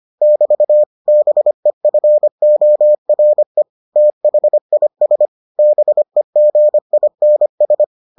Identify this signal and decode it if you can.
Morse code, heard as a keyed tone, about 25 wpm, 600 Hz. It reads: <BT> BEFORE THIS BEGINS